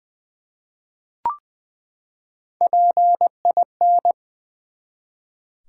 Morse code, heard as a keyed tone, about 20 wpm, 700 Hz. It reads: PIN
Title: Three Letter Words - 20wpm